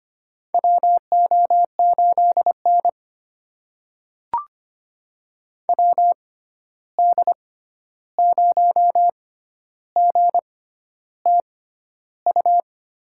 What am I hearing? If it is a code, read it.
Morse code, heard as a keyed tone, about 25 wpm, spaced as for 11 wpm, 700 Hz. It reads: WO8N WD0GTU
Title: USA Call Signs - 6x Character Spacing - 25wpm